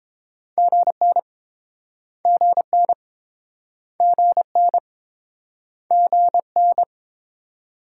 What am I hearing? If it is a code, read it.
Morse code, heard as a keyed tone, about 32 wpm, 700 Hz. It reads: GN GN GN GN